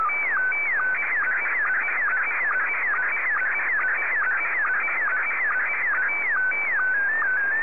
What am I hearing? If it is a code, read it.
SSTV (slow-scan television), Scottie 1.